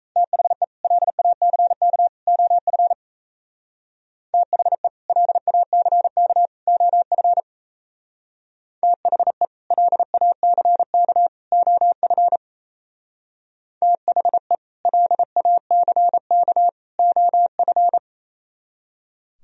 Morse code, heard as a keyed tone, about 36 wpm, 700 Hz. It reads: THE LACK OF THE LACK OF THE LACK OF THE LACK OF